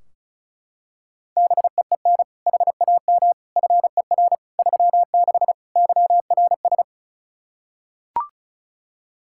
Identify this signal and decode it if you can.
Morse code, heard as a keyed tone, about 36 wpm, 700 Hz. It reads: BEEN HAM FER 36 YRS
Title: Practice QSO - Exchange 3 - More Info - 35wpm